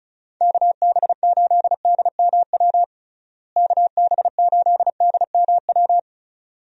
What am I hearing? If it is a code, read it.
Morse code, heard as a keyed tone, about 36 wpm, 700 Hz. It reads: KB8DMW KB8DMW
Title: USA Call Signs - 6 Characters - Repeated 2x - 35wpm